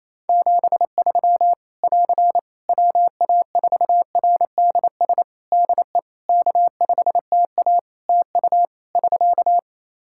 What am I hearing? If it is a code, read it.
Morse code, heard as a keyed tone, about 28 wpm, 700 Hz. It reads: 73 <AR> WA4RDS DE K5TA TU <SK>
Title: Practice QSO - Exchange 4 - Ending - 28wpm